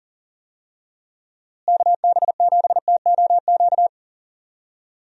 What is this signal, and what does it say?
Morse code, heard as a keyed tone, about 39 wpm, 700 Hz. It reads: KB7TOQ
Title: USA Call Signs - 6 Characters - Repeated 2x - 40wpm